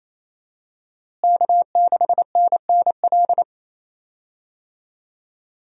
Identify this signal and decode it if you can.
Morse code, heard as a keyed tone, about 28 wpm, 700 Hz. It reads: K6NNL
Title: USA Call Signs - 5 Characters - Repeated 3x - 28wpm